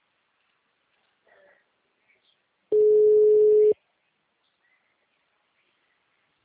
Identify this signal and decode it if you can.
Ringback tone (Europe, 425 Hz), with speech.